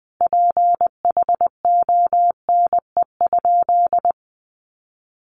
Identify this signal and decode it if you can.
Morse code, heard as a keyed tone, about 20 wpm, 700 Hz. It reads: PHONE?
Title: Sentences from Top 1100 Words 20wpm